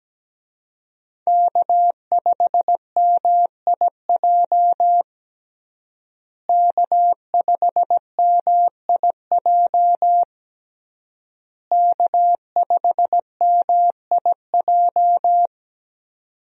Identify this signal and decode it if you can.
Morse code, heard as a keyed tone, about 17 wpm, 700 Hz. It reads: K5MIJ K5MIJ K5MIJ
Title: USA Call Signs - 5 Characters - Repeated 3x - 17wpm